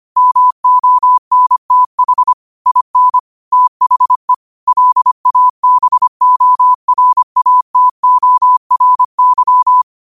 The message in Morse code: MONTH IN THE LABORATORY

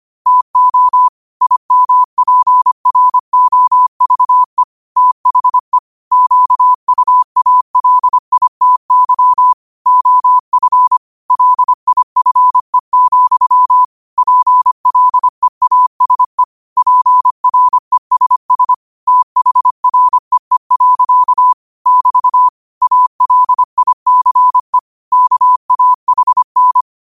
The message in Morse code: TO IMPROVE THE QUALITY OF LIFE, PLEASE PRESS THREE. <BT> ALICE KAHN

